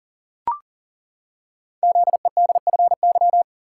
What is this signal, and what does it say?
Morse code, heard as a keyed tone, about 39 wpm, 690 Hz. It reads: ZEDFY
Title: Code Groups - 5 Letters - 40wpm